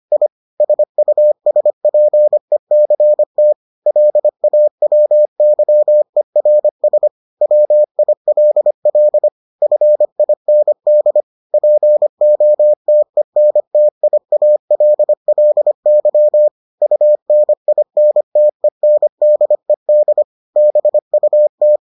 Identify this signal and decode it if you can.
Morse code, heard as a keyed tone, about 25 wpm, 600 Hz. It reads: I SUSPECT LAWYERS WILL FIND POTENTIALLY UNINTENDED BUT